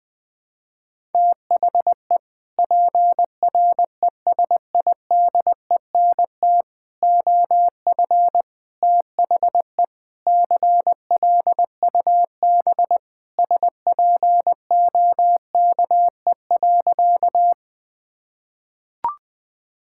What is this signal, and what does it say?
Morse code, heard as a keyed tone, about 20 wpm, 700 Hz. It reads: THE PRESIDENT OF THE CLUB SPOKE.
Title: Sentences with 4-Word Phrases - 20wpm